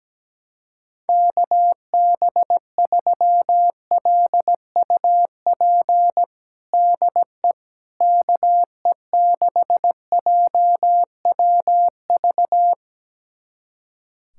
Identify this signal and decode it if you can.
Morse code, heard as a keyed tone, about 17 wpm, 700 Hz. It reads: KB3LUP DE KE6JWV
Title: USA Callsign DE Callsign 17wpm